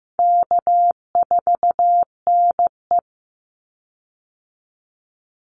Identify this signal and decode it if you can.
Morse code, heard as a keyed tone, about 15 wpm, 700 Hz. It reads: K4NE
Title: Call Signs Repeated 3x - 15wpm